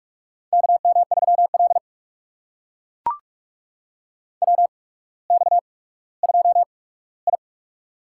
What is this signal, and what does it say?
Morse code, heard as a keyed tone, about 44 wpm, spaced as for 19 wpm, 690 Hz. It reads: KM3L WX2I
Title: USA Call Signs - 4 Characters - 8x Character Spacing - 45wpm